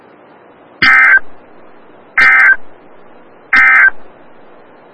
Emergency Alert System (SAME) tones, heard as harsh buzzing bursts. It is heard from 0.7 to 4.0 s.
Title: Des Moines, IA NOAA Weather Radio - WXL57